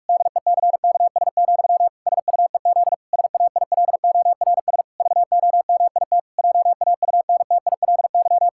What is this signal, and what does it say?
Morse code, heard as a keyed tone, about 46 wpm, 700 Hz. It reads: DECKS, SUEZ SAILORS VOMIT JAUNTILY
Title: Pangram Challenge v2 - 45wpm